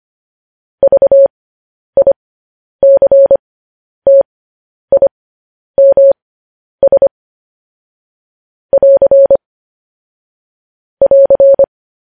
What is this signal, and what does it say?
Morse code, heard as a keyed tone, about 25 wpm, spaced as for 10 wpm, 550 Hz. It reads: VICTIMS <AR> <AR>
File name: podcast / News headlines-1x-25x10